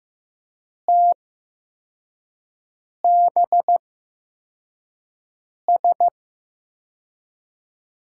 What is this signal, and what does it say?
Morse code, heard as a keyed tone, about 15 wpm, spaced as for 4 wpm, 700 Hz. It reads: TBS
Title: USA Call Signs - 6 Characters - 8x Character Spacing - 15wpm